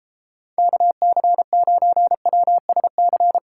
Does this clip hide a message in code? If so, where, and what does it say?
Yes: Morse code from 0.6 to 3.4 s, heard as a keyed tone, about 35 wpm, 700 Hz. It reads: KC9WSC